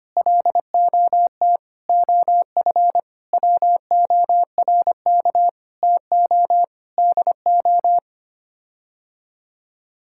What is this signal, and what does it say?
Morse code, heard as a keyed tone, about 25 wpm, 700 Hz. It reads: LOT OF WORK TO DO
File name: Sets of 9 Words - 25wpm